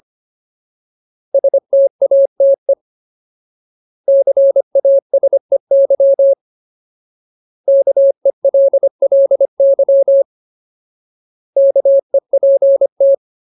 Morse code, heard as a keyed tone, about 25 wpm, 550 Hz. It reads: STATE CASEY KELLY KEPT
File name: Ultimate Word List / 25WPM Ultimate